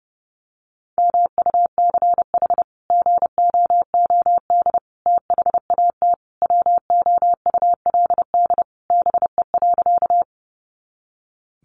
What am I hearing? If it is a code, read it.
Morse code, heard as a keyed tone, about 28 wpm, 700 Hz. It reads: MUCH GOOD THAT WOULD BE.
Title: Sentences from Top 200 Words 30wpm